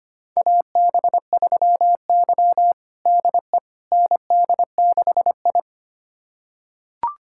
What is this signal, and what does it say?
Morse code, heard as a keyed tone, about 25 wpm, 700 Hz. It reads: AB3Y DE ND6I
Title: USA Callsign DE Callsign - 4 Characters - 25wpm